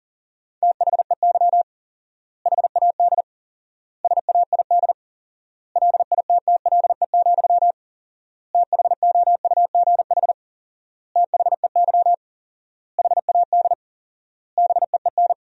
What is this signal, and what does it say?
Morse code, heard as a keyed tone, about 40 wpm, 700 Hz. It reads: THEY HAD SAID LITTLE, THOUGH THEY HAD BEEN
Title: 4x Word-Spacing - Sentences from Top 400 Words - 40wpm